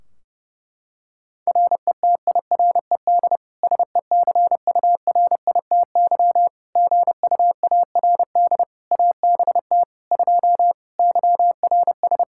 Morse code, heard as a keyed tone, about 29 wpm, 700 Hz. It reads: RETIRED SECURITY GUARD ABT 2 YRS
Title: Practice QSO - Exchange 3 - More Info - 30wpm